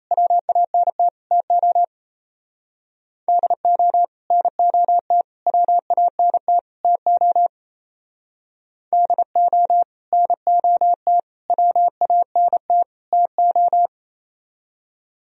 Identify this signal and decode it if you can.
Morse code, heard as a keyed tone, about 34 wpm, 700 Hz. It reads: WANT TO DO NOT WANT TO DO NOT WANT TO
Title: Speed-Racing - Sets of 4 Words 28wpm